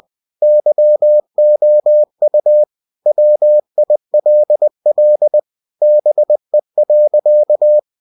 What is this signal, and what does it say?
Morse code, heard as a keyed tone, about 20 wpm, 600 Hz. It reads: YOU WILL BE.
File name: cw-books / winnie-the-pooh / winnie-20wpm-page-0072